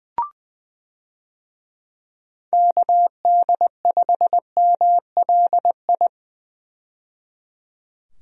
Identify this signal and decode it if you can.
Morse code, heard as a keyed tone, about 20 wpm, 700 Hz. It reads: KD5MLI
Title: USA Call Signs - 6 Characters - 20wpm